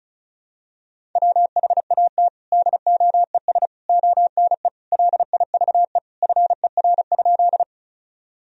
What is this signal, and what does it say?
Morse code, heard as a keyed tone, about 34 wpm, 700 Hz. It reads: WHAT DOES ONE LIVE FER?
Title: Sentences from Top 200 Words and Abbreviations 35wpm